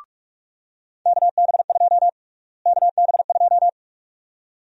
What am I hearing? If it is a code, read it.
Morse code, heard as a keyed tone, about 46 wpm, 700 Hz. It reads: KB2 KB2
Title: USA Call Sign Prefixes - Repeated 2x - 45wpm